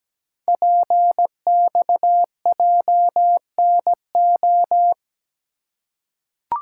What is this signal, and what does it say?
Morse code, heard as a keyed tone, about 17 wpm, 700 Hz. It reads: PXJNO
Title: Code Groups - 5 Letters - 17wpm